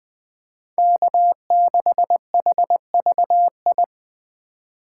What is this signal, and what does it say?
Morse code, heard as a keyed tone, about 20 wpm, 700 Hz. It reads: K6HVI
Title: USA Call Signs - 5 Characters - 3x Character Spacing - 20wpm